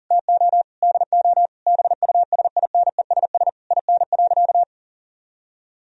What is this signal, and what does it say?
Morse code, heard as a keyed tone, about 40 wpm, 700 Hz. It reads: TO DO BUSINESS IN.
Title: Sentences from Top 1000 Words - Review Part 3 - 40wpm